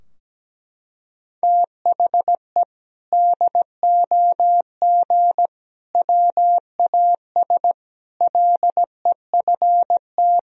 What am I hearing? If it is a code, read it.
Morse code, heard as a keyed tone, about 17 wpm, 700 Hz. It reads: THE DOG WAS LEFT
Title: Sentences from Top 1000 Words - Review Part 2 - 17wpm